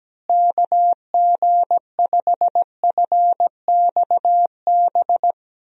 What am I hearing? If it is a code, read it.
Morse code, heard as a keyed tone, about 17 wpm, 700 Hz. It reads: KG5FXB
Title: USA Call Signs - 6 Characters - Repeated 3x - 17wpm